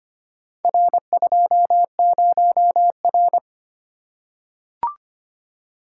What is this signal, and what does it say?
Morse code, heard as a keyed tone, about 25 wpm, 700 Hz. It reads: R20R
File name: practice / 25wpm / Four Letter-Number Combinations - 25wpm